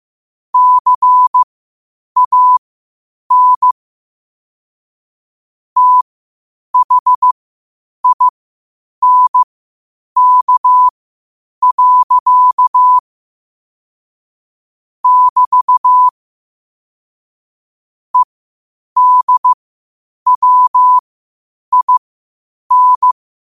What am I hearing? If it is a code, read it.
Morse code, heard as a keyed tone, about 15 wpm, spaced as for 8 wpm, 1000 Hz. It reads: CAN THINK. <BT> EDWIN
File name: qotdfiles / 8wpm